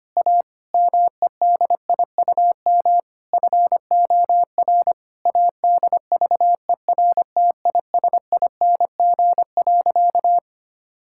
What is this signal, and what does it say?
Morse code, heard as a keyed tone, about 25 wpm, 700 Hz. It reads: A MEDIUM FOR ADVERTISING.
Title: Sentences from Top 1000 Words - Review Part 6 - 25wpm